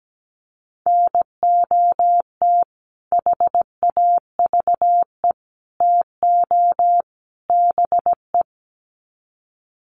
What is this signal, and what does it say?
Morse code, heard as a keyed tone, about 17 wpm, 700 Hz. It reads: NOT HAVE TO BE
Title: Sets of 4 Words 17wpm